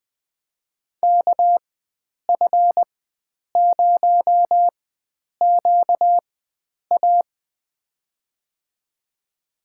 Morse code, heard as a keyed tone, about 20 wpm, spaced as for 9 wpm, 700 Hz. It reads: KF0QA
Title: USA Call Signs - 5 Characters - 4x Character Spacing - 20wpm